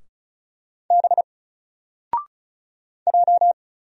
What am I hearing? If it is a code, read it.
Morse code, heard as a keyed tone, about 36 wpm, 700 Hz. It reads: B J
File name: Single-Letter-Number-35wpm